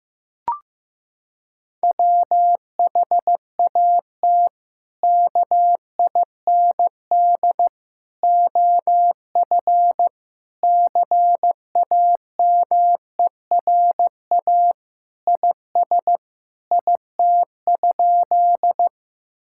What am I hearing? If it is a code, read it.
Morse code, heard as a keyed tone, about 15 wpm, 700 Hz. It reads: WHAT KIND OF CAMERA IS IT?
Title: Sentences from Top 1000 Words - Review Part 5 - 15wpm